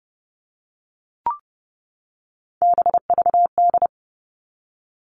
Morse code, heard as a keyed tone, about 30 wpm, 700 Hz. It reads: BVD
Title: Three Challenging Letter-Number Combinations 30wpm